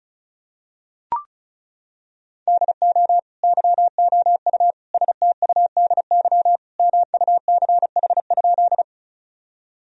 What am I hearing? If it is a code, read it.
Morse code, heard as a keyed tone, about 36 wpm, 690 Hz. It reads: DO YOU STUDY MUCH?
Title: Sentences from Top 1000 Words - Review Part 1 - 35wpm